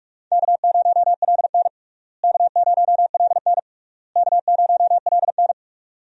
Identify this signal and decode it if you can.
Morse code, heard as a keyed tone, about 41 wpm, 700 Hz. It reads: K0LN K0LN K0LN
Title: USA Call Signs - 4 Characters - Repeated 3x - 45wpm